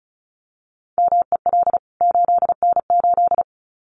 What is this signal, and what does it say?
Morse code, heard as a keyed tone, about 34 wpm, 700 Hz. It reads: MEL 8N8
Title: CWT Exchange 35wpm